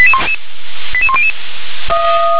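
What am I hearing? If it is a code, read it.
Five-tone selective call (ZVEI), heard as quick runs of five tones, tones of about 70 ms. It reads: 88100
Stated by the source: ZVEI five-tone call received by SDR.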